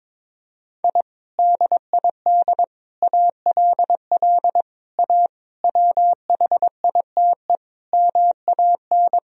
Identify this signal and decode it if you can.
Morse code, heard as a keyed tone, about 21 wpm, 700 Hz. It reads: I DID ALL A WHITE MAN
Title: Sentences from Top 1000 Words - Review Part 1 - 22wpm